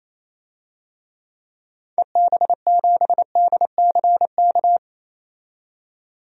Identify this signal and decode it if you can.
Morse code, heard as a keyed tone, about 28 wpm, 700 Hz. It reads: EB7DCK
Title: DXCC-100 Call Signs - 6 Characters - 28wpm